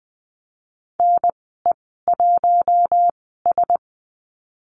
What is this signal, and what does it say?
Morse code, heard as a keyed tone, about 20 wpm, 700 Hz. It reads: N E 1 S
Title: 076 Lesson 13g - Letter L - Call Signs 20wpm